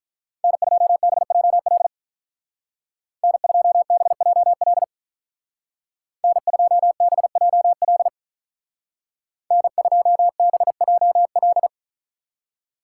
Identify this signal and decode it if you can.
Morse code, heard as a keyed tone, about 43 wpm, 700 Hz. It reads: N2BJL N2BJL N2BJL N2BJL
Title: Speed Racing - USA Call Signs 35wpm